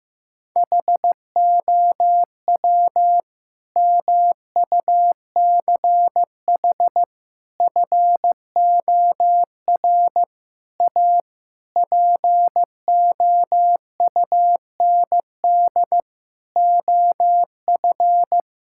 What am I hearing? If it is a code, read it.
Morse code, heard as a keyed tone, about 15 wpm, 700 Hz. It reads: HOW MUCH FOR A POUND OF
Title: Sentences from Top 1000 Words - Review Part 5 - 15wpm